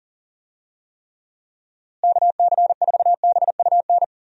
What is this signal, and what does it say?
Morse code, heard as a keyed tone, about 39 wpm, 700 Hz. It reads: KC4BUN
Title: USA Call Signs - 6 Characters - Repeated 2x - 40wpm